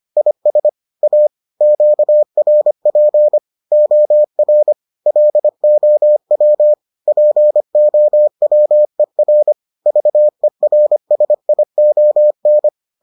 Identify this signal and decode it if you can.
Morse code, heard as a keyed tone, about 25 wpm, 600 Hz. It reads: IS A QRP OR LOW POWER VERSION